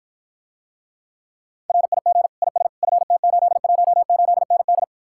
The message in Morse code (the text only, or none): RIG IS FT818ND